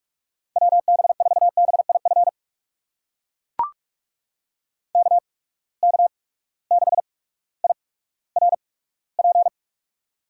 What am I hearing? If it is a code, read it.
Morse code, heard as a keyed tone, about 42 wpm, spaced as for 19 wpm, 700 Hz. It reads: WB4BIF KK6IRP
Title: USA Call Signs - 6 Characters - 8x Character Spacing - 45wpm